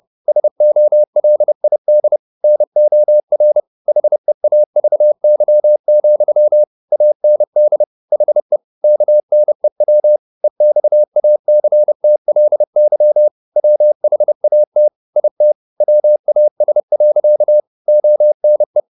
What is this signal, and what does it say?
Morse code, heard as a keyed tone, about 30 wpm, 600 Hz. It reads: SOLID NOR HEAVY, AND HE KNEW EXACTLY WHAT IT WAS. ONE
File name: the old man and the sea / the old man and the sea 1 30wpm